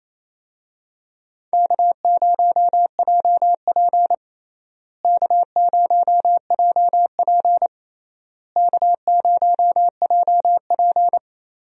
Morse code, heard as a keyed tone, about 28 wpm, 700 Hz. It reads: K0JP K0JP K0JP
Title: USA Call Signs - 4 Characters - Repeated 3x - 28wpm